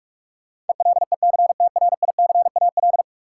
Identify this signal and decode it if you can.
Morse code, heard as a keyed tone, about 46 wpm, 700 Hz. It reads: ELECTRICAL
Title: Sentences from Top 1000 Words - Review Part 6 - New Word Drill - 45wpm